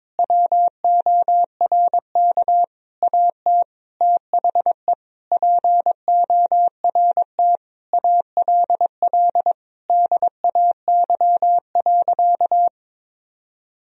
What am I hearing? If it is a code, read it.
Morse code, heard as a keyed tone, about 22 wpm, 700 Hz. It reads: WORK AT THE PORT ALL DAY.
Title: Sentences from Top 1000 Words - Review Part 1 - 22wpm